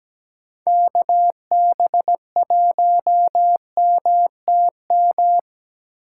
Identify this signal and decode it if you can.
Morse code, heard as a keyed tone, about 17 wpm, 700 Hz. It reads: KB1MTM
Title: USA Call Signs - 6 Characters - 3x Character Spacing - 17wpm